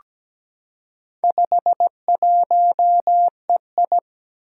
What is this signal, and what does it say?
Morse code, heard as a keyed tone, about 17 wpm, 700 Hz. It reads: 51EI
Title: Four Challenging Letter-Number Combinations 17wpm